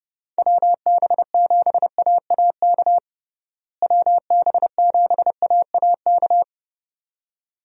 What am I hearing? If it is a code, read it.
Morse code, heard as a keyed tone, about 32 wpm, 700 Hz. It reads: WB7AAK WB7AAK
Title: USA Call Signs - 6 Characters - Repeated 2x - 30wpm